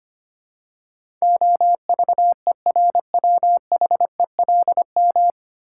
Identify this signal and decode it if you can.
Morse code, heard as a keyed tone, about 25 wpm, 700 Hz. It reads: OVERWHELM
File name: practice / 25wpm / 9 Letter Words - First 3 Letters Given - 25wpm